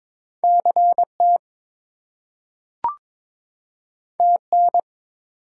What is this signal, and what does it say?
Morse code, heard as a keyed tone, about 22 wpm, 700 Hz. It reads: CT TN
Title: US States and Canadian Provinces 22wpm